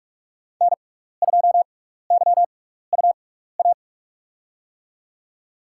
Morse code, heard as a keyed tone, about 46 wpm, spaced as for 16 wpm, 690 Hz. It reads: N2YUA
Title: USA Call Signs - 5 Characters - 6x Character Spacing - 45wpm